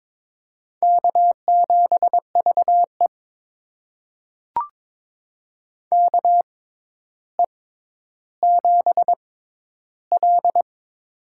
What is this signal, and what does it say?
Morse code, heard as a keyed tone, about 22 wpm, spaced as for 11 wpm, 700 Hz. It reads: K7VE KE7L
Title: USA Call Signs - 4 Characters - 6x Character Spacing - 22wpm